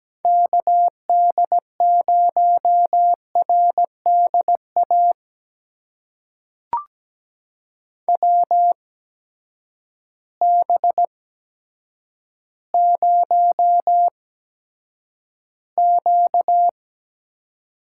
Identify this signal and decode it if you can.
Morse code, heard as a keyed tone, about 17 wpm, spaced as for 9 wpm, 700 Hz. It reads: KD0RDA WB0Q
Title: USA Call Signs - 6 Characters - 8x Character Spacing - 17wpm